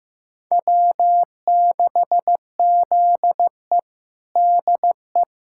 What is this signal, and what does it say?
Morse code, heard as a keyed tone, about 15 wpm, 700 Hz. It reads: W6ZE DE
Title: USA Callsign DE Callsign - 4 Characters - 15wpm